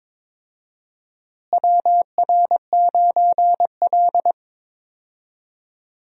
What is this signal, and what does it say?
Morse code, heard as a keyed tone, about 22 wpm, 700 Hz. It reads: WR9L
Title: USA Call Signs - 4 Characters - Repeated 3x - 22wpm